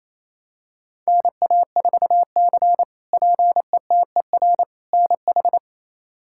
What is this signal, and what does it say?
Morse code, heard as a keyed tone, about 28 wpm, 700 Hz. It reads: NA4C PETER NH
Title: K1USN SST QSO Practice - 28wpm